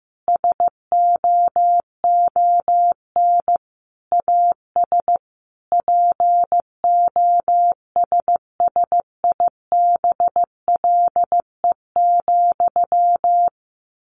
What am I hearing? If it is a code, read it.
Morse code, heard as a keyed tone, about 15 wpm, 700 Hz. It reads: SOON AS POSSIBLE,
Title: Sentences from Top 800 Words 15wpm